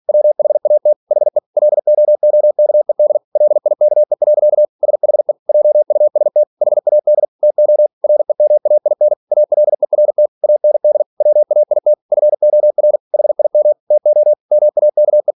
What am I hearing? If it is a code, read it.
Morse code, heard as a keyed tone, about 46 wpm, 600 Hz. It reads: WHAT HE LOOKED LIKE. SHE JUST HAD TO REMAIN ALERT AND WAIT FOR HIM TO MAKE
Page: Never 47wpm